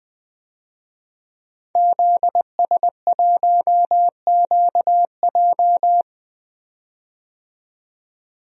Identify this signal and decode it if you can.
Morse code, heard as a keyed tone, about 20 wpm, 700 Hz. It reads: ZS1QJ
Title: DXCC-100 Call Signs - 5 Characters - 20wpm